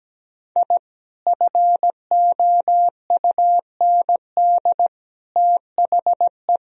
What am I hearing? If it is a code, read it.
Morse code, heard as a keyed tone, about 17 wpm, 700 Hz. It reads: I FOUND THE
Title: Sentences from Top 1000 Words - Review Part 3 - 17wpm